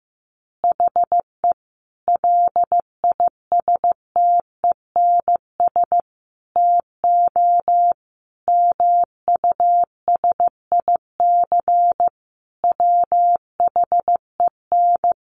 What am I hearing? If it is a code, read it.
Morse code, heard as a keyed tone, about 15 wpm, 700 Hz. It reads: HE LISTENS TO MUSIC WHEN
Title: Additional Easy Phrases 15wpm